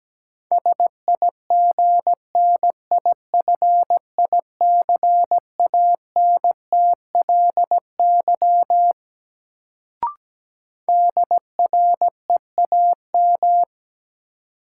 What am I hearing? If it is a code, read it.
Morse code, heard as a keyed tone, about 17 wpm, 700 Hz. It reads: SIGNIFICANTLY DREAM
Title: Sentences from Top 1500 Words - New Word Drill - 17wpm